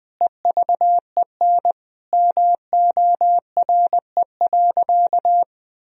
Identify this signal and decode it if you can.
Morse code, heard as a keyed tone, about 20 wpm, 700 Hz. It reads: EVEN MORE.
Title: Sentences from Top 1000 Words - Review Part 1 - 20wpm